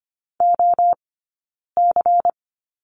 Morse code, heard as a keyed tone, about 25 wpm, 700 Hz. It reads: O C
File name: course / 25:10wpm / 092 Lesson 16d - Letter c - Two Characters 25wpm